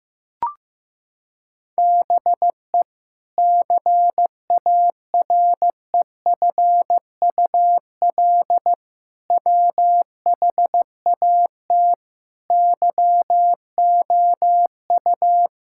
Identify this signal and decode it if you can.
Morse code, heard as a keyed tone, about 15 wpm, 700 Hz. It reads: BE CAREFUL WHAT YOU
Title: Sentences from Top 1600 Words 15wpm